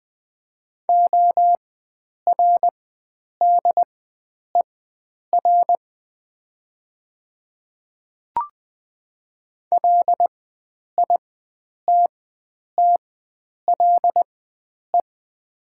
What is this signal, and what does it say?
Morse code, heard as a keyed tone, about 20 wpm, spaced as for 9 wpm, 700 Hz. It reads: ORDER LITTLE
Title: 4x Character Spacing - Top 500 Words - 20wpm